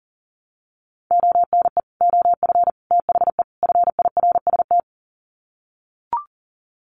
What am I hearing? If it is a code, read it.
Morse code, heard as a keyed tone, about 40 wpm, 700 Hz. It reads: ONE OF THE FIRST
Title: Sets of 4 Words 40wpm